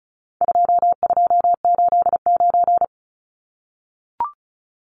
Morse code, heard as a keyed tone, about 34 wpm, 700 Hz. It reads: 2289
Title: Numbers - 4-Digit 35wpm